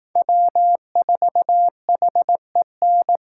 Morse code, heard as a keyed tone, about 18 wpm, 700 Hz. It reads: W4HEN
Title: Speed Racing - USA 1x3 Call Signs 15wpm